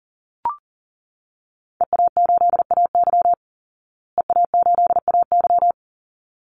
Morse code, heard as a keyed tone, about 37 wpm, 700 Hz. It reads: EA8AY EA8AY
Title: Call Signs Repeated 3x - 40wpm